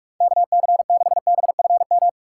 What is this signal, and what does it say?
Morse code, heard as a keyed tone, about 46 wpm, 700 Hz. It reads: KC6BFM
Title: USA Call Signs - 6 Characters - 3x Character Spacing - 45wpm